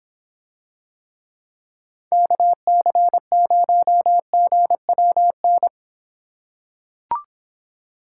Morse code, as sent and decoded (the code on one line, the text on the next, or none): -.- -.-. ----- --. .-- -.
KC0GWN